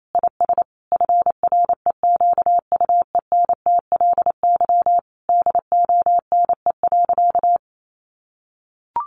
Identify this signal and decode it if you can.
Morse code, heard as a keyed tone, about 28 wpm, 700 Hz. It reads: IS FREQUENTLY DONE.
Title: Sentences from Top 1300 Words 28wpm